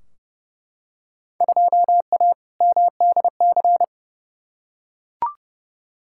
Morse code, heard as a keyed tone, about 30 wpm, 700 Hz. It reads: 2A MDC
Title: Field Day Exchanges - 30wpm